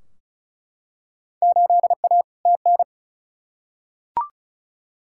Morse code, heard as a keyed tone, about 36 wpm, 700 Hz. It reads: 8A TN
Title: Field Day Exchanges Encore - 35wpm